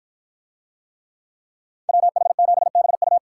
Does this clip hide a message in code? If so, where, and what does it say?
Yes: Morse code from 1.9 to 3.2 s, heard as a keyed tone, about 53 wpm, 700 Hz. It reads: WH7BU